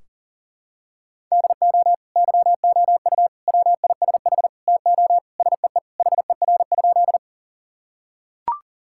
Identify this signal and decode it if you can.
Morse code, heard as a keyed tone, about 40 wpm, 700 Hz. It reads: DO YOU WISH TO SEE HER?
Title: Sentences from Top 1600 Words 40wpm